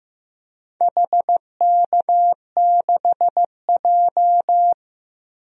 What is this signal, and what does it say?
Morse code, heard as a keyed tone, about 15 wpm, 700 Hz. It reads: HK6J
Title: DXCC-100 Call Signs - 4 Characters - Repeated 3x - 15wpm